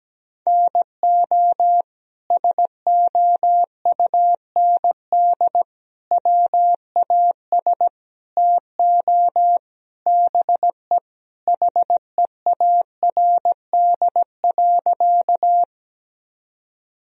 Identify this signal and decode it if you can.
Morse code, heard as a keyed tone, about 17 wpm, 700 Hz. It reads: NO SOUND WAS TO BE HEARD.
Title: Sentences from Top 1000 Words - Review Part 1 - 17wpm